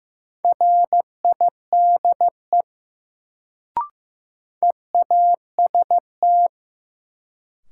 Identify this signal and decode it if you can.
Morse code, heard as a keyed tone, about 15 wpm, 700 Hz. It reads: RIDE EAST
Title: Sentences from Top 1200 Words - New Word Drill Alt - 15wpm